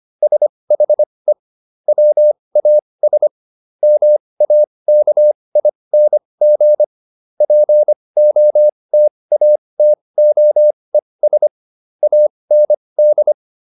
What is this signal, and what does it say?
Morse code, heard as a keyed tone, about 25 wpm, spaced as for 20 wpm, 600 Hz. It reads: SHE WAS MAKING POTATOES AND